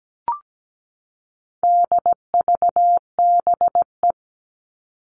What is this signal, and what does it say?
Morse code, heard as a keyed tone, about 17 wpm, 700 Hz. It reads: DVBE
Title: All Cut Numbers - 4-Digit 17wpm